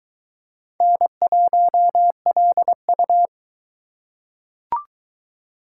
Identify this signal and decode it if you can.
Morse code, heard as a keyed tone, about 22 wpm, 700 Hz. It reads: N1LU